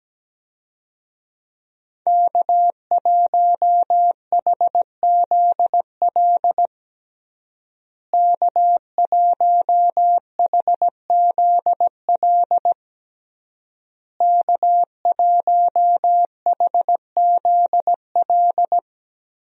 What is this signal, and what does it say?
Morse code, heard as a keyed tone, about 17 wpm, 700 Hz. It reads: K1HZL K1HZL K1HZL
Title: USA Call Signs - 5 Characters - Repeated 3x - 17wpm